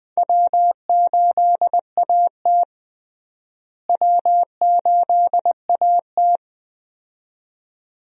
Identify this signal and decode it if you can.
Morse code, heard as a keyed tone, about 20 wpm, 700 Hz. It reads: W8AT W8AT
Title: USA Call Signs - 4 Characters - Repeated 2x - 20wpm